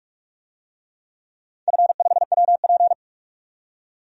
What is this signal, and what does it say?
Morse code, heard as a keyed tone, about 41 wpm, 700 Hz. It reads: F5WP
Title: DXCC-100 Call Signs - 4 Characters - 45wpm